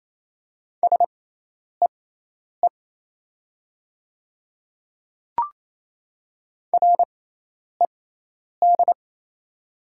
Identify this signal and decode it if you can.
Morse code, heard as a keyed tone, about 28 wpm, spaced as for 10 wpm, 700 Hz. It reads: SEE RED
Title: 6x Character Spacing - Top 500 Words - 28wpm